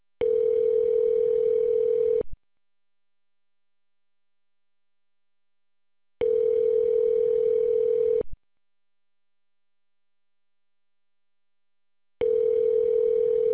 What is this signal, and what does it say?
Ringback tone (North America), 440 + 480 Hz.